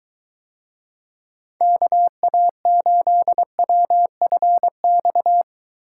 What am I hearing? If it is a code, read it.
Morse code, heard as a keyed tone, about 23 wpm, 700 Hz. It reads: KA8WFX